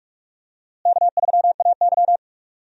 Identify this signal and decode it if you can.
Morse code, heard as a keyed tone, about 42 wpm, 700 Hz. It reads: K3AY
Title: USA Call Signs - 4 Characters - Repeated 2x - 45wpm